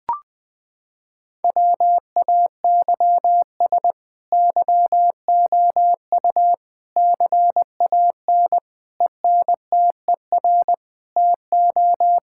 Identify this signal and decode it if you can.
Morse code, heard as a keyed tone, about 21 wpm, 700 Hz. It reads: WAYS YOU CAN ENTER TO
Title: Sets of 8 Words - 20wpm